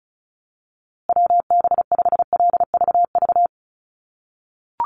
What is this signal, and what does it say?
Morse code, heard as a keyed tone, about 34 wpm, 700 Hz. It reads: WB5LVV
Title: USA Call Signs Repeated 2x 35wpm